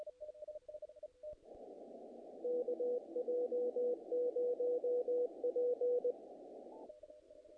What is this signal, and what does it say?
Morse code, heard as a keyed tone, about 20 wpm, 490 Hz. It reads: KJ0P
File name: contests / naqp-cw-k5tr-op-k5pi- -1800